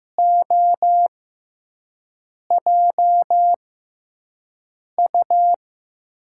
Morse code, heard as a keyed tone, about 15 wpm, spaced as for 5 wpm, 700 Hz. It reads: OJU
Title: USA Call Signs - 6x Character Spacing - 15wpm